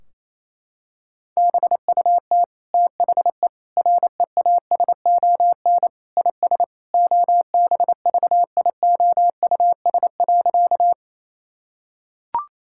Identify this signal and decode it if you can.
Morse code, heard as a keyed tone, about 28 wpm, 700 Hz. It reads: BUT THE REASON IS OBVIOUS.
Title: Sentences from Top 1600 Words 28wpm